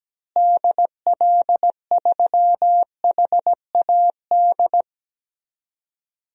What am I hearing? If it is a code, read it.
Morse code, heard as a keyed tone, about 17 wpm, 700 Hz. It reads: DL3HAD
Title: DXCC-100 Call Signs - 6 Characters - Repeated 2x - 17wpm